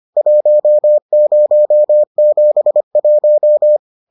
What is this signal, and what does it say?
Morse code, heard as a keyed tone, about 25 wpm, 600 Hz. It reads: 1071